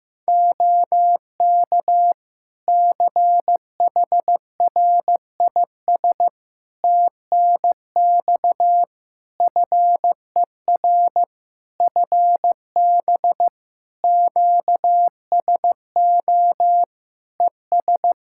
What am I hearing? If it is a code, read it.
Morse code, heard as a keyed tone, about 15 wpm, 700 Hz. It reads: OK CHRIS TNX FER FB QSO ES
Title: Practice QSO - Exchange 4 - Ending - 15wpm